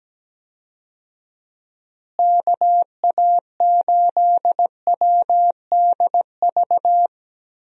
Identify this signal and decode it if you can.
Morse code, heard as a keyed tone, about 17 wpm, 700 Hz. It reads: KA8WDV
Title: USA Call Signs - 6 Characters - Repeated 3x - 17wpm